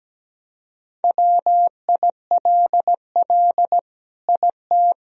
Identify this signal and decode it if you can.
Morse code, heard as a keyed tone, about 17 wpm, 700 Hz. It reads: WILL IT
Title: Sentences from Top 1000 Words - Review Part 2 - 17wpm